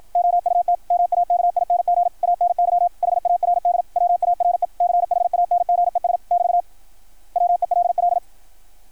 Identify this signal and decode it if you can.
Morse code, heard as a keyed tone, about 53 wpm, 660 Hz. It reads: OUT MAKING ANY HARDWARE CHANGES <BT> WELL
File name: archived / CRSnet